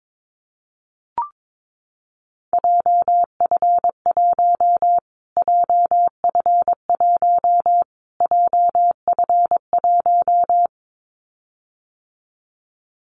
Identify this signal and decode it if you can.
Morse code, heard as a keyed tone, about 22 wpm, 700 Hz. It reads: JF1 JF1 JF1
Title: Callsign Prefixes for Top 25 DXCC - Repeated 3x - 22wpm